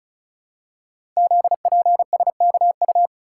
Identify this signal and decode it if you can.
Morse code, heard as a keyed tone, about 36 wpm, 700 Hz. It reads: ZPSKU
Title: Code Groups - 5 Letter:Numbers - 35wpm